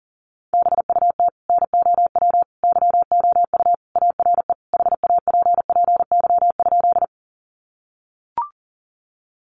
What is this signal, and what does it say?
Morse code, heard as a keyed tone, about 40 wpm, 700 Hz. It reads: BUT NOW YOU ARE HAPPY?
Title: Sentences from Top 1000 Words 40wpm